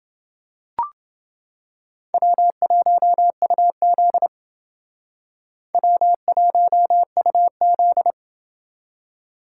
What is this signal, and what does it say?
Morse code, heard as a keyed tone, about 29 wpm, 700 Hz. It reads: W1UZ W1UZ